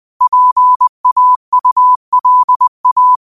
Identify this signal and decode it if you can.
Morse code, heard as a keyed tone, about 20 wpm, 1000 Hz. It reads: PAULA
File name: qotdfiles / 20wpm